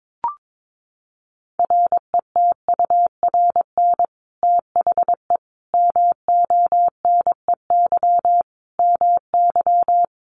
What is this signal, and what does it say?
Morse code, heard as a keyed tone, about 22 wpm, 700 Hz. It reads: RETURN THE MONEY MY
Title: Sentences from Top 1200 Words 22wpm